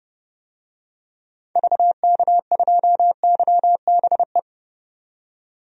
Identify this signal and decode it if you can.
Morse code, heard as a keyed tone, about 30 wpm, 700 Hz. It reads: VK2YBE
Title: DXCC-100 Call Signs - 6 Characters - 30wpm